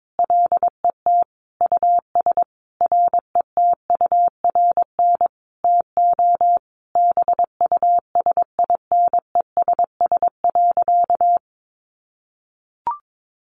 Morse code, heard as a keyed tone, about 22 wpm, 700 Hz. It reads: LET US RETURN TO BUSINESS.
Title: Sentences from Top 800 Words 22wpm